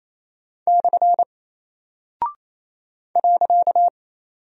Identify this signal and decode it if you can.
Morse code, heard as a keyed tone, about 28 wpm, 700 Hz. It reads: / .
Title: Single Character (Letter:Number:Punctuation:Prosign) Alt 28wpm